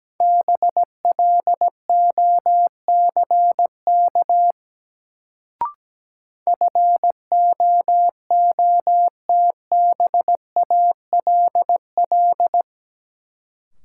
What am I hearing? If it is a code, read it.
Morse code, heard as a keyed tone, about 17 wpm, 700 Hz. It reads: BLOCK FOOTBALL
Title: Sentences from Top 1600 Words - New Word Drill Alt - 17wpm